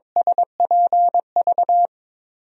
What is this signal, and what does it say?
Morse code, heard as a keyed tone, about 22 wpm, 700 Hz. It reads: SPV
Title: USA Call Signs - 6 Characters - 22wpm